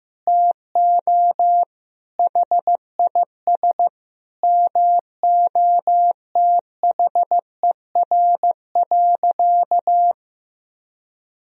Morse code, heard as a keyed tone, about 15 wpm, 700 Hz. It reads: TO HIS MOTHER.
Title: Sentences from Top 1000 Words - Review Part 3 - 15wpm